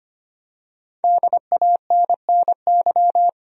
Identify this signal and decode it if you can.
Morse code, heard as a keyed tone, about 25 wpm, 700 Hz. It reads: DANNY
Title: USA Common Names - 25wpm